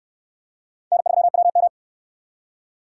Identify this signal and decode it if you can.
Morse code, heard as a keyed tone, about 69 wpm, 700 Hz. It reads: N3WM